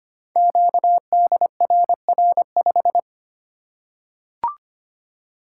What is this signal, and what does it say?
Morse code, heard as a keyed tone, about 25 wpm, 700 Hz. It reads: QDRR5
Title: Code Groups - 5 Letter:Numbers - 25wpm